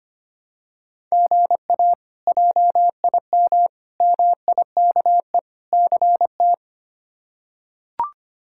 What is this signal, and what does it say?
Morse code, heard as a keyed tone, about 25 wpm, 700 Hz. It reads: GA JIM MIKE CT
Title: K1USN SST QSO Practice - 25wpm